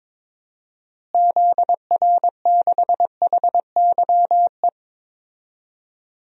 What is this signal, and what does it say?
Morse code, heard as a keyed tone, about 22 wpm, 700 Hz. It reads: ZR6HYE
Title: DXCC-100 Call Signs - 6 Characters - Repeated 2x - 22wpm